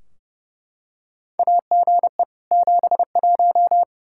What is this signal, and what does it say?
Morse code, heard as a keyed tone, about 30 wpm, 700 Hz. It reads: AGE 71
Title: Practice QSO - Exchange 3 - More Info - 30wpm